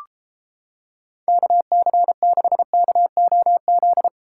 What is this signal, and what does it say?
Morse code, heard as a keyed tone, about 33 wpm, 700 Hz. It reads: KC6KOZ